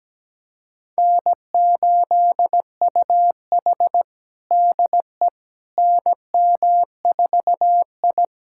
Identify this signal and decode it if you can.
Morse code, heard as a keyed tone, about 17 wpm, 700 Hz. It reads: N8UH DE NM4I
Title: USA Callsign DE Callsign - 4 Characters - 17wpm